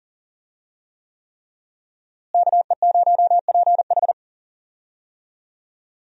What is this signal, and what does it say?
Morse code, heard as a keyed tone, about 40 wpm, 700 Hz. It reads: KE0PH
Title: USA Call Signs - 5 Characters - Repeated 3x - 40wpm